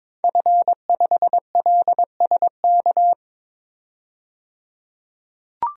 Morse code, heard as a keyed tone, about 21 wpm, 700 Hz. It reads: F5LSK
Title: DXCC-100 Call Signs - 5 Characters - 22wpm